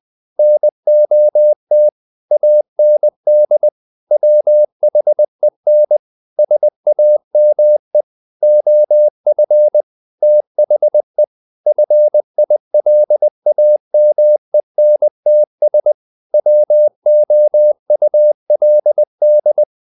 Morse code, heard as a keyed tone, about 20 wpm, 600 Hz. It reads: NOT AND WHEN SAME OF THE FILAMENTS WOULD
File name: the old man and the sea / the old man and the sea part 1 20wpm